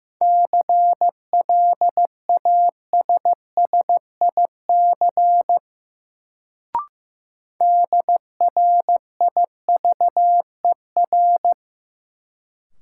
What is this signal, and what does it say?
Morse code, heard as a keyed tone, about 15 wpm, 700 Hz. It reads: CLASSIC DRIVER
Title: Sentences from Top 1000 Words - New Word Drill - 15wpm